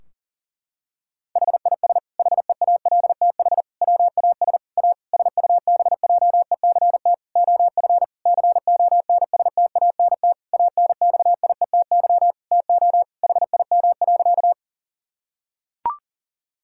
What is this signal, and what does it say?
Morse code, heard as a keyed tone, about 40 wpm, 700 Hz. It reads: HIS HEALTH WAS A SUBJECT OF CONSTANT ANXIETY TO HIM.
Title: Sentences from Top 1600 Words 40wpm